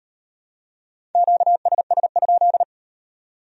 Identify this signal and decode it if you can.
Morse code, heard as a keyed tone, about 38 wpm, 700 Hz. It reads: QSS?